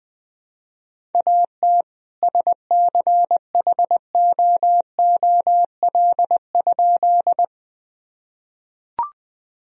Morse code, heard as a keyed tone, about 21 wpm, 700 Hz. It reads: AT SCHOOL?
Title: Sentences from Top 1000 Words - Review Part 1 - 20wpm